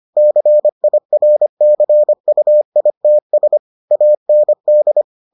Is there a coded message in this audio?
Yes: Morse code from 0.2 to 5.0 s, heard as a keyed tone, about 25 wpm, 600 Hz. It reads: CIRCUITS AND